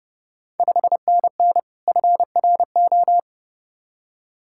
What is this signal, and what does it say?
Morse code, heard as a keyed tone, about 30 wpm, 700 Hz. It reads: 5NN FRO
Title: Ontario QSO Party Exchange 30wpm